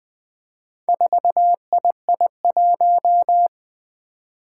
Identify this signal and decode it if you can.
Morse code, heard as a keyed tone, about 20 wpm, 700 Hz. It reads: 4II1
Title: Four Challenging Letter-Number Combinations 20wpm